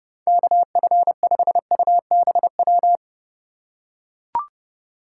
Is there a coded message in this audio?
Yes: Morse code from 0.3 to 3.0 s, heard as a keyed tone, about 31 wpm, 700 Hz. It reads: KF5UBW